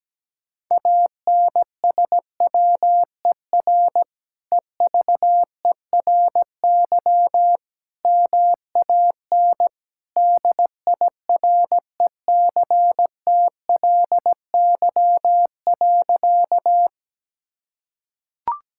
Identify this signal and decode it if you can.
Morse code, heard as a keyed tone, about 17 wpm, 700 Hz. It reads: ANSWER EVERY MAN DIRECTLY.
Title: Sentences from Top 1000 Words - Review Part 5 - 17wpm